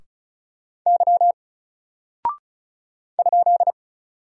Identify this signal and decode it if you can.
Morse code, heard as a keyed tone, about 35 wpm, 700 Hz. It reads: Y ?
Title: Single Character (Letter:Number:Punctuation:Prosign) 35wpm